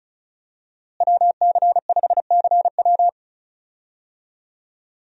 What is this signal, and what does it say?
Morse code, heard as a keyed tone, about 36 wpm, 700 Hz. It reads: WC5CW
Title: USA Call Signs - 5 Characters - Repeated 3x - 35wpm